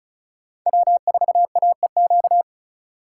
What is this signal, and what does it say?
Morse code, heard as a keyed tone, about 36 wpm, 700 Hz. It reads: W4AEQ
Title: USA Call Signs - 5 Characters - 4x Character Spacing - 35wpm